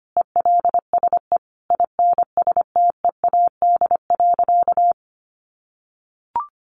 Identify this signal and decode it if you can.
Morse code, heard as a keyed tone, about 25 wpm, 700 Hz. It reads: ELSE INSTEAD.
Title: Sentences from Top 800 Words 25wpm